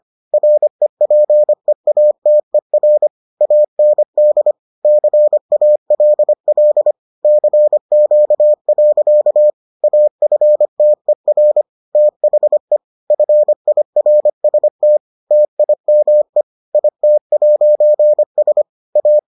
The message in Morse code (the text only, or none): REPEATER AND CALL CQ. AFTER THE FIRST TIME IT'S A